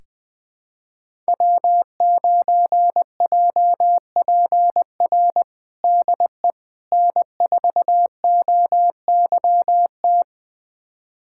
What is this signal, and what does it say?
Morse code, heard as a keyed tone, about 20 wpm, 700 Hz. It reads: W9JPR DE N4OYT
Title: USA Callsign DE Callsign - 5 Characters - 20wpm